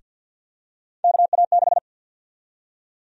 Morse code, heard as a keyed tone, about 48 wpm, 700 Hz. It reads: KA6
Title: USA Call Sign Prefixes - 50wpm